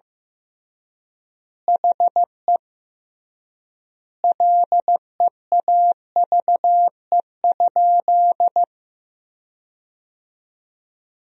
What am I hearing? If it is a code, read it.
Morse code, heard as a keyed tone, about 16 wpm, 700 Hz. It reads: HE LEAVE?
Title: 3x Word-Spacing - Sentences from Top 400 Words - 15wpm